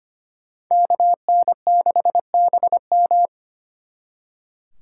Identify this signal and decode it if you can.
Morse code, heard as a keyed tone, about 32 wpm, 700 Hz. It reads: KN6BM